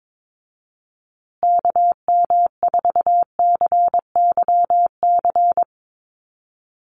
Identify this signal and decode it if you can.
Morse code, heard as a keyed tone, about 22 wpm, 700 Hz. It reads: KM4CYC
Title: USA Call Signs Repeated 3x 22wpm